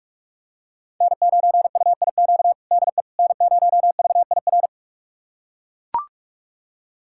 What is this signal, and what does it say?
Morse code, heard as a keyed tone, about 42 wpm, 700 Hz. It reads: N9UIQ DE N0VIR
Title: USA Callsign DE Callsign - 5 Characters - 45wpm